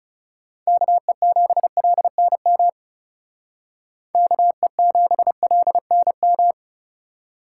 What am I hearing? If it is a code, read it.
Morse code, heard as a keyed tone, about 38 wpm, 700 Hz. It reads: KE7LNM KE7LNM